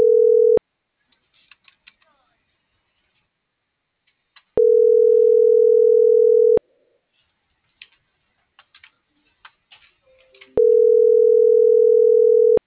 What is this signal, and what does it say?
Ringback tone (North America), 440 + 480 Hz.